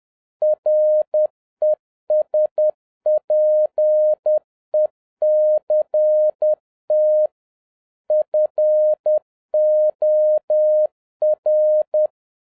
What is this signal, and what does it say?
Morse code, heard as a keyed tone, about 10 wpm, 610 Hz. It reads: RESPECT FOR
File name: 2011-07-mp3 / n21 GREETINGS FROM SINGAPORE 10wpm